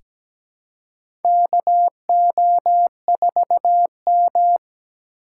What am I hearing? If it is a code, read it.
Morse code, heard as a keyed tone, about 17 wpm, 700 Hz. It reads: KO4M
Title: USA Call Signs - 4 Characters - 3x Character Spacing - 17wpm